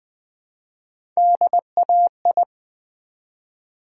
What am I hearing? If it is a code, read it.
Morse code, heard as a keyed tone, about 20 wpm, 700 Hz. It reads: DAI
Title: Build A Word - 1 Letter At A Time - 5-Letter Words - 20wpm